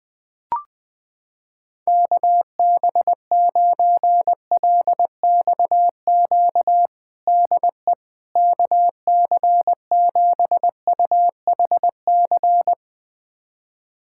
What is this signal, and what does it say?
Morse code, heard as a keyed tone, about 20 wpm, 700 Hz. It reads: KB9LXQ DE KC7UHC
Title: USA Callsign DE Callsign - 6 Characters - 20wpm